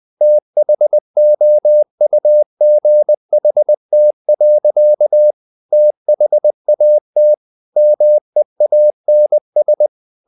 Morse code, heard as a keyed tone, about 20 wpm, 600 Hz. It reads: THOUGHT. THAT MEANS
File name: the old man and the sea part 2 20wpm